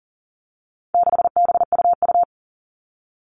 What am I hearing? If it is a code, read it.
Morse code, heard as a keyed tone, about 40 wpm, 700 Hz. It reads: 6BUU
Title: All Cut Numbers - 4-Digit 40wpm